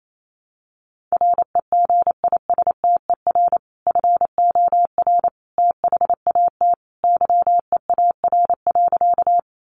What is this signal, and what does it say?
Morse code, heard as a keyed tone, about 28 wpm, 700 Hz. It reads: REGISTER FOR THAT YEAR.
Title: Sentences from Top 1200 Words 28wpm